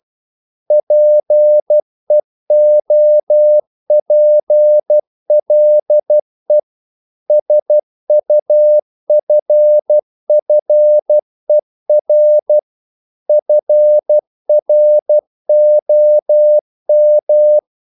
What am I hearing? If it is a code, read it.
Morse code, heard as a keyed tone, about 12 wpm, 600 Hz. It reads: PEOPLE SUFFER FROM
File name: CW / Practice / EnglishQuotations0112